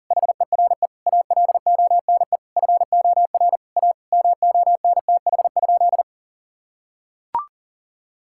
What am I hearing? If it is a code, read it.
Morse code, heard as a keyed tone, about 40 wpm, 700 Hz. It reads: HERE ALONE FOR A MONTH?
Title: Sentences from Top 1000 Words - Review Part 5 - 40wpm